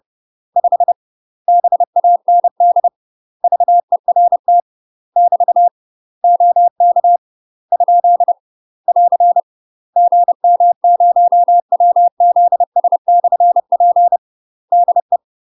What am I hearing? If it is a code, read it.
Morse code, heard as a keyed tone, about 30 wpm, 700 Hz. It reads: 5 BAND VERT <BT> OK ? <AR> GM0WZS/P DE
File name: UK MASTER QSOs TRAINING / 30WPM / UK MASTER QSOs 0047